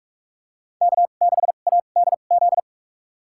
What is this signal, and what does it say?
Morse code, heard as a keyed tone, about 44 wpm, spaced as for 32 wpm, 700 Hz. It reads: K6ADZ
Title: USA Call Signs - 2x Character Spacing - 45wpm